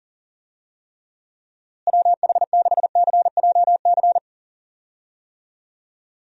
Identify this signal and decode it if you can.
Morse code, heard as a keyed tone, about 40 wpm, 700 Hz. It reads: WH6CJC
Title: USA Call Signs - 6 Characters - Repeated 3x - 40wpm